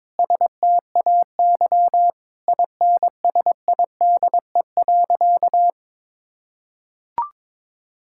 Morse code, heard as a keyed tone, about 22 wpm, 700 Hz. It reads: STAY INSIDE.
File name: practice / 22wpm / Sentences with 2-Word Phrases - 22wpm